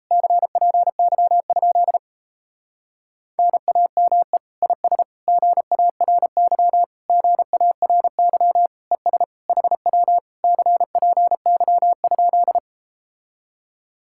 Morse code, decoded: CPY? NAME IS GARY GARY ES HW CPY?

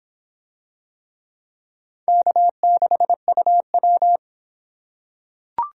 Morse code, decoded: K6UW

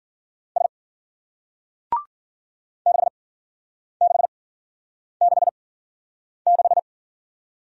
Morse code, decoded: S 6666

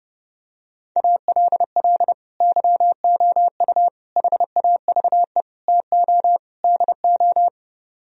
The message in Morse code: ALL YOU HAVE TO DO

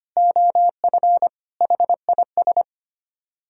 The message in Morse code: OF HIS

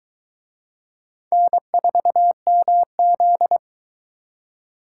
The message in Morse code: N4MZ

